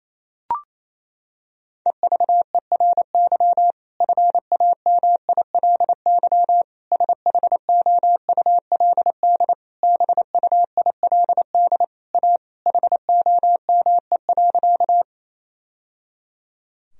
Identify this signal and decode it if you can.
Morse code, heard as a keyed tone, about 28 wpm, 700 Hz. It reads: EVERY FAMILY SHOULD BUILD A HOME.
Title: Sentences from Top 1000 Words - Review Part 1 - 28wpm